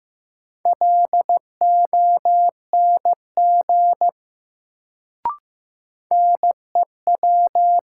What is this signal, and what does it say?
Morse code, heard as a keyed tone, about 15 wpm, 700 Hz. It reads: LONG NEW
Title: Sets of 2 Words - New Word Drill - 15wpm